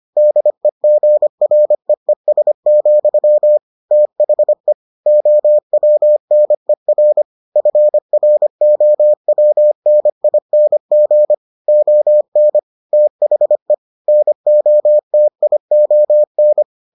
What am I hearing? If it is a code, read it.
Morse code, heard as a keyed tone, about 25 wpm, 600 Hz. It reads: DEGREES, THE OWNER FROWNING ON THE NOTION